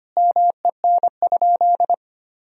Morse code, heard as a keyed tone, about 25 wpm, 700 Hz. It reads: MEN?
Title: Sentences from Top 1000 Words - Review Part 1 - 25wpm